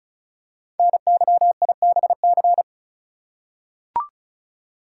Morse code, heard as a keyed tone, about 36 wpm, 690 Hz. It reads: NYIBC
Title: Code Groups - 5 Letter:Numbers - 35wpm